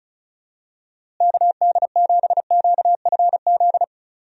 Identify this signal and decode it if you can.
Morse code, heard as a keyed tone, about 36 wpm, 700 Hz. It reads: KD7QFZ
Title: USA Call Signs - 6 Characters - Repeated 3x - 35wpm